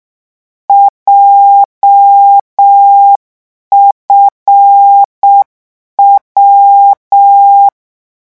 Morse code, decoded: JFW